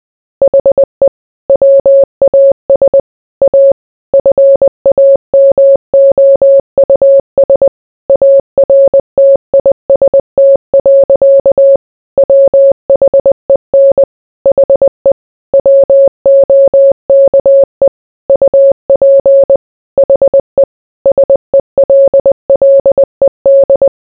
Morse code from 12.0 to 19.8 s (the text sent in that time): WHEN HE WOKE UP